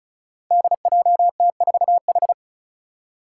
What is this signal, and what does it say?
Morse code, heard as a keyed tone, about 36 wpm, 700 Hz. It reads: DJT4H
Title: Five Challenging Letter-Number Combinations 35wpm